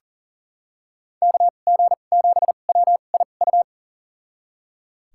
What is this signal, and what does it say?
Morse code, heard as a keyed tone, about 39 wpm, spaced as for 28 wpm, 700 Hz. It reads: KG7WIU
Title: USA Call Signs - 6 Characters - 2x Character Spacing - 40wpm